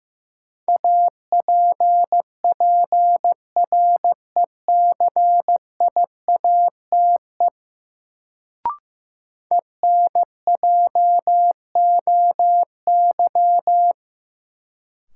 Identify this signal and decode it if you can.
Morse code, heard as a keyed tone, about 15 wpm, 700 Hz. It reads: APPRECIATE ENJOY
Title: Sentences from Top 1300 Words - New Word Drill Alt - 15wpm